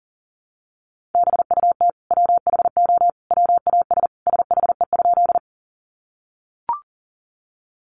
Morse code, heard as a keyed tone, about 40 wpm, 700 Hz. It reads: BUT WHO WAS SHE?
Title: Sentences from Top 100 Words 40wpm